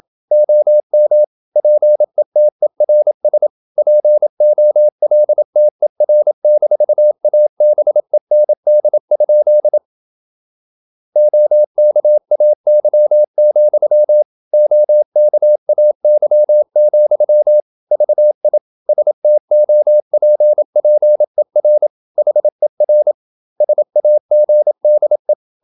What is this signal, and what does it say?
Morse code, heard as a keyed tone, about 27 wpm, 600 Hz. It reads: OM PETERS POLTER-ABEND? OKAY, OKAY, VI STOPPER HER SAGDE
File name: Ferie QRQ 27wpm